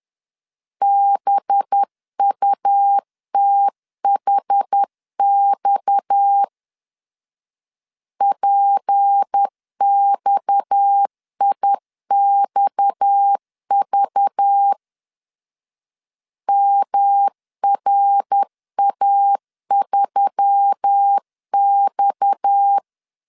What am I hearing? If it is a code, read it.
Morse code, heard as a keyed tone, about 10 wpm, 800 Hz. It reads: BUTHX PXIXV MRA3X
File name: LEZIONI / 6 PKX0 / 6 PKX 10wpm 2426